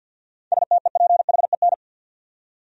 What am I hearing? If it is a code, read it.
Morse code, heard as a keyed tone, about 53 wpm, 700 Hz. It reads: STEPHEN
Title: USA Common Names - 50wpm